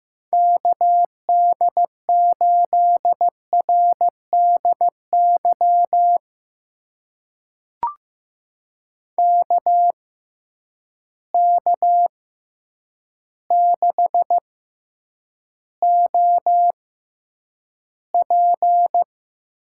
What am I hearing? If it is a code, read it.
Morse code, heard as a keyed tone, about 15 wpm, spaced as for 8 wpm, 700 Hz. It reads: KD8RDY KK6OP
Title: USA Call Signs - 6 Characters - 6x Character Spacing - 15wpm